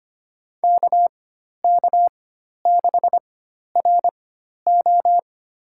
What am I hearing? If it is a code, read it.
Morse code, heard as a keyed tone, about 25 wpm, spaced as for 12 wpm, 700 Hz. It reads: KK6RO
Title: USA Call Signs - 5 Characters - 4x Character Spacing - 25wpm